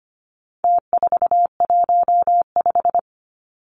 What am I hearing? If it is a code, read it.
Morse code, heard as a keyed tone, about 25 wpm, 700 Hz. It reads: T415
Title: Numbers - Contest Style - 4-Digit 25wpm